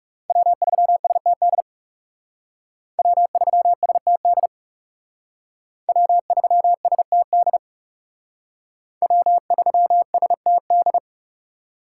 Morse code, heard as a keyed tone, about 36 wpm, 700 Hz. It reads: W3STD W3STD W3STD W3STD
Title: Speed Racing - USA 1x3 Call Signs 30wpm